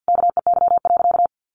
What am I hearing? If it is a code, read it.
Morse code, heard as a keyed tone, about 50 wpm, 700 Hz. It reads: KEY.
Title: Sentences from Top 1400 Words 50wpm